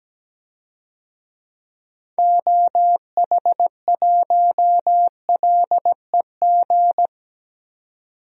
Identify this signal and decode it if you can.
Morse code, heard as a keyed tone, about 17 wpm, 700 Hz. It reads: OH1LEG
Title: DXCC-100 Call Signs - 6 Characters - 17wpm